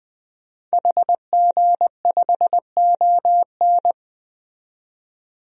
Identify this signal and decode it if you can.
Morse code, heard as a keyed tone, about 20 wpm, 700 Hz. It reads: HG5ON
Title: DXCC-100 Call Signs - 5 Characters - 20wpm